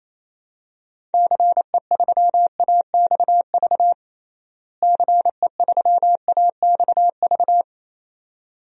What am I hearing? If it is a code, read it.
Morse code, heard as a keyed tone, about 28 wpm, 700 Hz. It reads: CE3AXV CE3AXV
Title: DXCC-100 Call Signs - 6 Characters - Repeated 2x - 28wpm